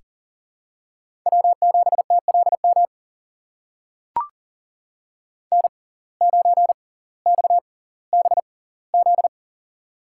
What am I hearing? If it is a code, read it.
Morse code, heard as a keyed tone, about 39 wpm, spaced as for 20 wpm, 700 Hz. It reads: W7TLM N9XBZ
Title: USA Call Signs - 5 Characters - 6x Character Spacing - 40wpm